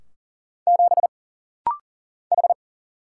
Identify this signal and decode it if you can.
Morse code, heard as a keyed tone, about 39 wpm, 700 Hz. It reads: 7 H
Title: Single Letter-Number Alt - Rapid-Fire 40wpm